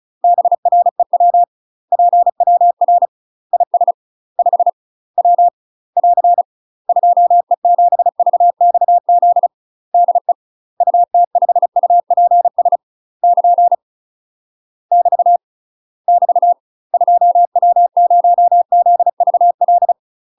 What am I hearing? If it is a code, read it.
Morse code, heard as a keyed tone, about 34 wpm, 700 Hz. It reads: BREW PWR IS 5 W <AR> 2E7VXZ DE UT5UPS <KN> <BT> <BT> 2W0ZVL
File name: UK MASTER QSOs TRAINING / 35WPM / UK MASTER QSOs 0001